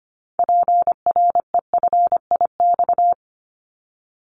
Morse code, heard as a keyed tone, about 25 wpm, 700 Hz. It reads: PREFIX
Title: Callsign Prefixes for Top 100 DXCC - 25wpm